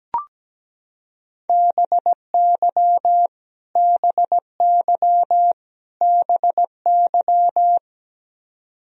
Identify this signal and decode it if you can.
Morse code, heard as a keyed tone, about 17 wpm, 700 Hz. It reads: BY BY BY
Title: Top 100 Words Repeated 17wpm